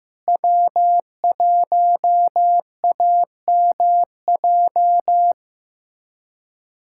Morse code, heard as a keyed tone, about 16 wpm, 700 Hz. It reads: W1AMJ
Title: USA Callsign DE Callsign - 5 Characters - 15wpm